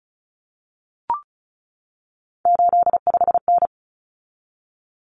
Morse code, heard as a keyed tone, about 34 wpm, 700 Hz. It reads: 85N
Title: Common Cut Numbers - 3-Digit 35wpm